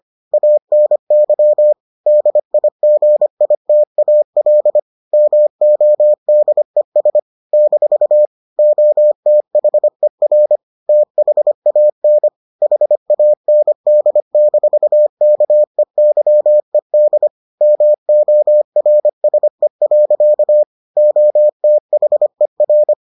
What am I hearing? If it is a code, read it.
Morse code, heard as a keyed tone, about 25 wpm, 600 Hz. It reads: ANY DIGITAL MODES - OTHER THAN HAND-KEYED MORSE. OTHER